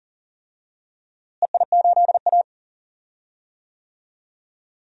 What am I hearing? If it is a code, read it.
Morse code, heard as a keyed tone, about 40 wpm, 700 Hz. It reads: EI8A
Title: DXCC-100 Call Signs - 4 Characters - Repeated 3x - 40wpm